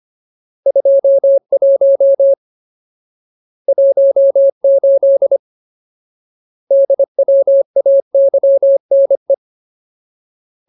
Morse code, heard as a keyed tone, about 25 wpm, 550 Hz. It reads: 21 18 DWAYNE
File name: Ultimate Word List / 25WPM Ultimate